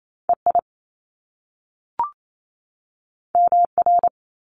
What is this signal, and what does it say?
Morse code, heard as a keyed tone, about 28 wpm, 700 Hz. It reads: E I M R
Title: Top Two Letter Combinations in Words 28wpm